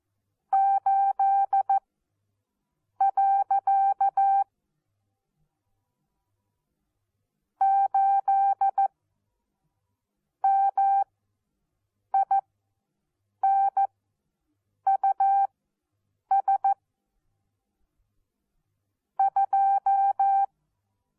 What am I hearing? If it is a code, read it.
Morse code, heard as a keyed tone, about 14 wpm, spaced as for 6 wpm, 780 Hz. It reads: 8. 8 MINUS 2
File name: Onlinelehrgang / morsen / DJ4UF Morsen14